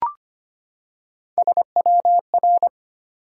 Morse code, heard as a keyed tone, about 25 wpm, 700 Hz. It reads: SWR
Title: Top 180 QSO Elements 25wpm